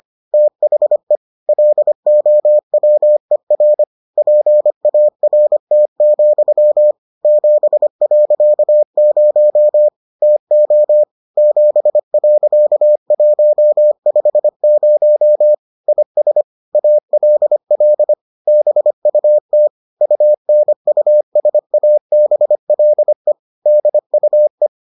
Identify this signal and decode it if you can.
Morse code, heard as a keyed tone, about 25 wpm, 600 Hz. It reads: THE LOWER PART, 7.0 TO 7.150 IS ALL BUT UNUSABLE DUE